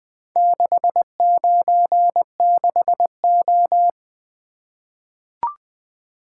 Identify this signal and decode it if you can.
Morse code, heard as a keyed tone, about 20 wpm, 700 Hz. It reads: 696O
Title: Four Letter-Number Combinations - 20wpm